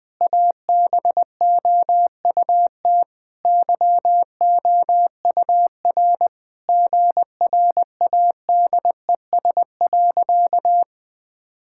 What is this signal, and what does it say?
Morse code, heard as a keyed tone, about 20 wpm, 700 Hz. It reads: ABOUT YOUR GRADES.
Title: Sentences with 4-Word Phrases - 20wpm